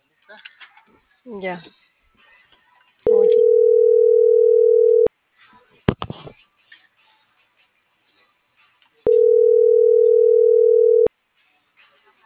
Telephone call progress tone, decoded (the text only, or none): ringback tone